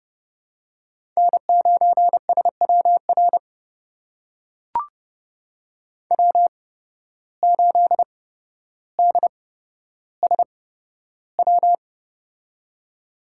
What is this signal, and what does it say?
Morse code, heard as a keyed tone, about 30 wpm, spaced as for 13 wpm, 700 Hz. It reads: N9SWR W8DSW
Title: USA Call Signs - 5 Characters - 8x Character Spacing - 30wpm